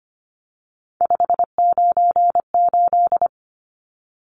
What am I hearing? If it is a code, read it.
Morse code, heard as a keyed tone, about 25 wpm, 700 Hz. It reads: 598
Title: Numbers - 3-Digit 25wpm